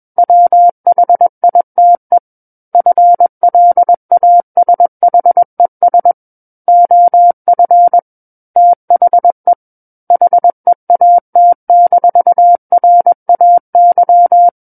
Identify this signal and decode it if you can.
Morse code, heard as a keyed tone, about 21 wpm, 700 Hz. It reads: WHITE FLASHES OF THE HEAT-RAY
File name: Book One / 12-Chapter 12 - 21 WPM